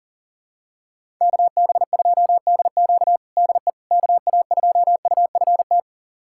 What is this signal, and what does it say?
Morse code, heard as a keyed tone, about 37 wpm, 700 Hz. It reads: KB2DQ DE KA2UFT
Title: USA Callsign DE Callsign 40wpm